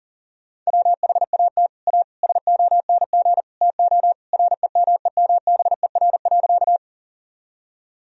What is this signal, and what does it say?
Morse code, heard as a keyed tone, about 40 wpm, 700 Hz. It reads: WHAT A SONG TO REMEMBER.
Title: Sentences from Top 1000 Words - Review Part 2 - 40wpm